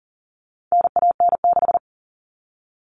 Morse code, heard as a keyed tone, about 40 wpm, 700 Hz. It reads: NAN6
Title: All Cut Numbers - 4-Digit 40wpm